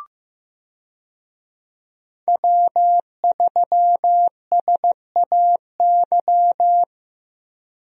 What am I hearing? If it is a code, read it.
Morse code, heard as a keyed tone, about 15 wpm, 700 Hz. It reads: W3SAY
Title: USA Call Signs - 5 Characters - Repeated 2x - 15wpm